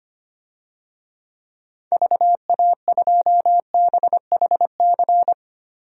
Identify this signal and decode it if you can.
Morse code, heard as a keyed tone, about 25 wpm, 700 Hz. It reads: VA2BHC
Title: DXCC-100 Call Signs - 6 Characters - Repeated 2x - 25wpm